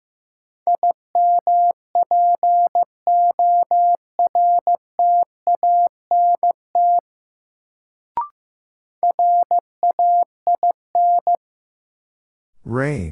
Morse code, heard as a keyed tone, about 15 wpm, 700 Hz. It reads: IMPORTANT RAIN
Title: Sentences from Top 600 Words - New Word Drill - 15wpm